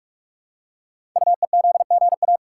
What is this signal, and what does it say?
Morse code, heard as a keyed tone, about 44 wpm, 690 Hz. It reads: UEZGA
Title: Code Groups - 5 Letter:Numbers - 45wpm